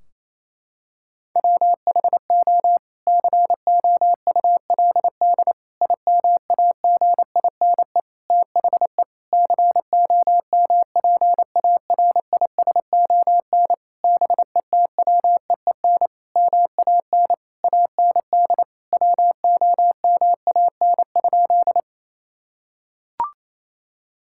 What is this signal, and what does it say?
Morse code, heard as a keyed tone, about 28 wpm, 700 Hz. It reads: WHO COULD IMAGINE THE COMPARISON BETWEEN MAN AND WOMAN?
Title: Sentences from Top 1600 Words 28wpm